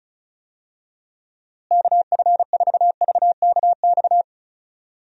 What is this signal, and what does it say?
Morse code, heard as a keyed tone, about 35 wpm, 700 Hz. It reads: KF4VKX
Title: USA Call Signs - 6 Characters - Repeated 2x - 35wpm